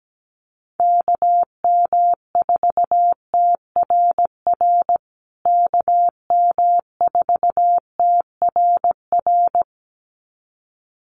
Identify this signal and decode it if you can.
Morse code, heard as a keyed tone, about 17 wpm, 700 Hz. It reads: KM4TRR KM4TRR
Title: USA Call Signs Repeated 2x 17wpm